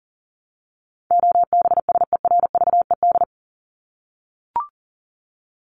Morse code, heard as a keyed tone, about 40 wpm, 700 Hz. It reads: OBSERVED
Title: ED Suffix - Top 100 Words with Base Words 40wpm